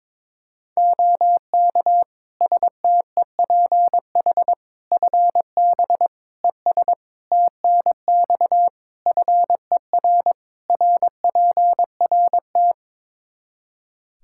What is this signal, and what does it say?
Morse code, heard as a keyed tone, about 22 wpm, 700 Hz. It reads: OK STEPH FB ES TNX FER RPRT
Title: Practice QSO - Exchange 2 - Info - 22wpm